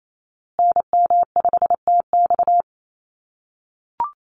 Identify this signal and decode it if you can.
Morse code, heard as a keyed tone, about 28 wpm, 700 Hz. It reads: NM5TX
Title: USA Call Signs Repeated 2x 28wpm